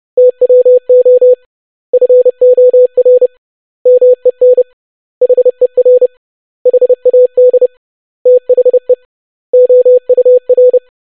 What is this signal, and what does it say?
Morse code, heard as a keyed tone, about 30 wpm, 500 Hz. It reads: TWO FOR MEN HER HAD THE OUR
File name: basic100-25wpm